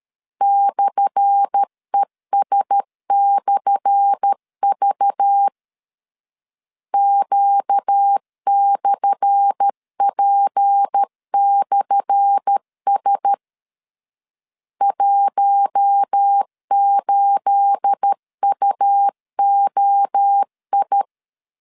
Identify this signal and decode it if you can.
Morse code, heard as a keyed tone, about 13 wpm, 800 Hz. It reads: /ES/V Q/P/S 18UOI
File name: LEZIONI / 10 SIGNS / 10A SLASH 12wpm 2314